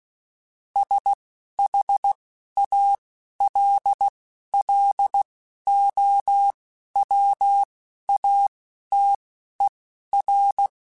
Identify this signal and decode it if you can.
Morse code, heard as a keyed tone, about 16 wpm, spaced as for 11 wpm, 800 Hz. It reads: SHALLOWATER